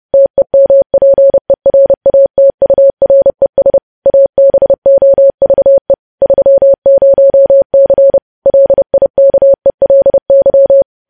Morse code, heard as a keyed tone, about 30 wpm, 550 Hz. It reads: TEMPERATURES ABOVE 30C LIKELY
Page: news headlines in morse code at 30 wpm